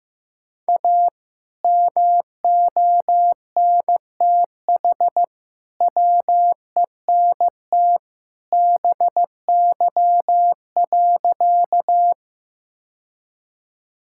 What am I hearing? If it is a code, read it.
Morse code, heard as a keyed tone, about 16 wpm, 700 Hz. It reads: A MONTH WENT BY.
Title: Sentences from Top 1000 Words - Review Part 5 - 15wpm